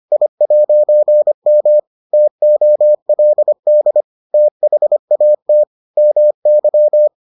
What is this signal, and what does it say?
Morse code, heard as a keyed tone, about 25 wpm, 600 Hz. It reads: I'M TOLD THAT MY